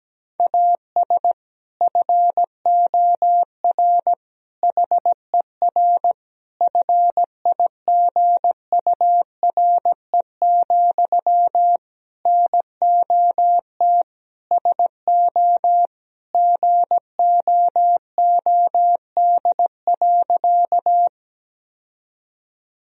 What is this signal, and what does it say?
Morse code, heard as a keyed tone, about 17 wpm, 700 Hz. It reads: AS FOR HER FIGURE, NOT SO GOOD.
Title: Sentences from Top 1000 Words - Review Part 2 - 17wpm